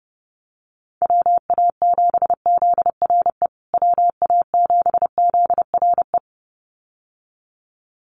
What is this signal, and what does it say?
Morse code, heard as a keyed tone, about 30 wpm, 700 Hz. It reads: WA7ZRE WA7ZRE
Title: USA Call Signs Repeated 2x 30wpm